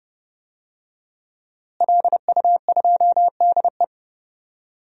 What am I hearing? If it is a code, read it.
Morse code, heard as a keyed tone, about 31 wpm, 700 Hz. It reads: LU2DE
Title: DXCC-100 Call Signs - 5 Characters - Repeated 3x - 30wpm